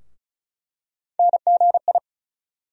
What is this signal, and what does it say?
Morse code, heard as a keyed tone, about 34 wpm, 700 Hz. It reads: NGI
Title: USA Call Sign Postfixes - 3 Letters 35wpm